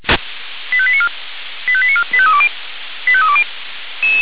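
Five-tone selective call (ZVEI), heard as quick runs of five tones, tones of about 70 ms. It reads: 85894 85894 85320 85320
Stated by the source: ZVEI five-tone call received with an SDR.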